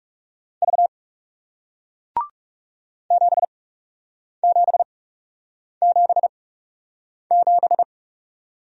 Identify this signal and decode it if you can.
Morse code, heard as a keyed tone, about 40 wpm, spaced as for 8 wpm, 700 Hz. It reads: V 7777